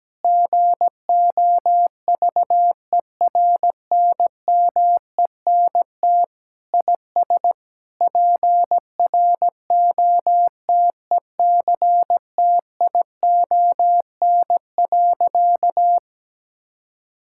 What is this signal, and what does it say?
Morse code, heard as a keyed tone, about 17 wpm, 700 Hz. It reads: GOVERNMENT IS PROTECTION.
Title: Sentences from Top 1600 Words 17wpm